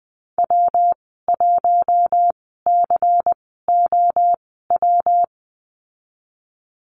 Morse code, heard as a keyed tone, about 19 wpm, spaced as for 14 wpm, 700 Hz. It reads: W1COW
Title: 161 Lesson 26g - Letter g - Call Signs 20wpm